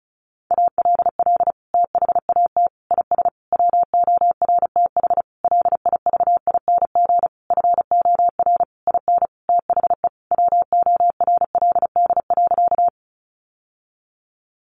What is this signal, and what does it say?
Morse code, heard as a keyed tone, about 34 wpm, 700 Hz. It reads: ALL THAT IS WORTH LIVING FOR IN THE WORLD.
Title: Sentences from Top 1100 Words 35wpm